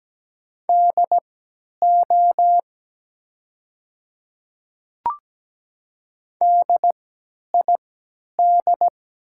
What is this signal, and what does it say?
Morse code, heard as a keyed tone, about 17 wpm, spaced as for 10 wpm, 700 Hz. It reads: DO DID
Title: 3x Character Spacing - Top 100 Words - 17wpm